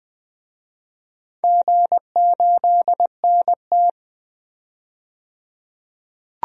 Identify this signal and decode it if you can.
Morse code, heard as a keyed tone, about 20 wpm, 700 Hz. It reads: G8NT
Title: DXCC-100 Call Signs - 4 Characters - 20wpm